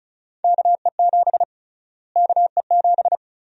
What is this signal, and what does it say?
Morse code, heard as a keyed tone, about 36 wpm, 700 Hz. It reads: KE7 KE7
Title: USA Call Sign Prefixes - Repeated 2x - 35wpm